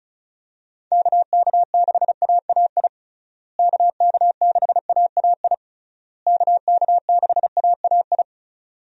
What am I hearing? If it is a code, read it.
Morse code, heard as a keyed tone, about 34 wpm, 700 Hz. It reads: KK6AAI KK6AAI KK6AAI
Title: USA Call Signs - 6 Characters - Repeated 3x - 35wpm